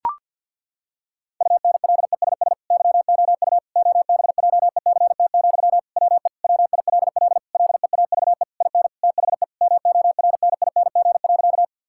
Morse code, heard as a keyed tone, about 48 wpm, 700 Hz. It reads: UNLESS YOU OBJECT, WE WILL LEAVE IN THE MORNING.
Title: Sentences from Top 1000 Words - Review Part 6 - 50wpm